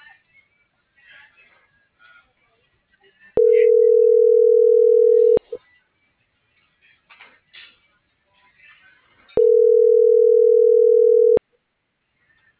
Ringback tone (North America), 440 + 480 Hz.